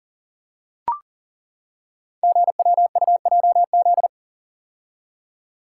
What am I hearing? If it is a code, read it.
Morse code, heard as a keyed tone, about 42 wpm, 700 Hz. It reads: GWUJZ
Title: Code Groups - 5 Letters - 40wpm